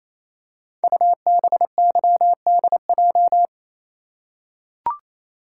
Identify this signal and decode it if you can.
Morse code, heard as a keyed tone, about 27 wpm, 700 Hz. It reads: UBYDJ
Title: Code Groups - 5 Letters - 28wpm